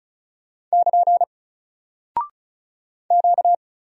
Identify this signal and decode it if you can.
Morse code, heard as a keyed tone, about 36 wpm, 700 Hz. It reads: <KN> Q
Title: Single Character (Letter:Number:Punctuation:Prosign) Alt 35wpm